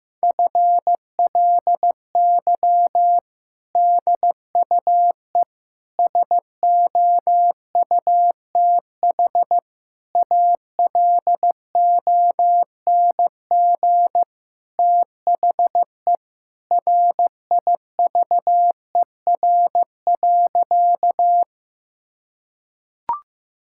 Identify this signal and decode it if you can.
Morse code, heard as a keyed tone, about 15 wpm, 700 Hz. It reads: FLY DUE SOUTH ALONG THE RIVER.
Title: Sentences from Top 1000 Words - Review Part 6 - 15wpm